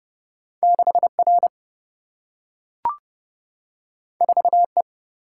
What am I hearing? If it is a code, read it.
Morse code, heard as a keyed tone, about 30 wpm, 700 Hz. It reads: 6R 4E
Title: Single-Letter Single-Number 30wpm